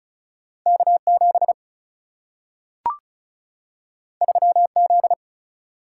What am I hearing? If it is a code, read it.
Morse code, heard as a keyed tone, about 36 wpm, 700 Hz. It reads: K7 3Z
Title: Single-Letter Single-Number 35wpm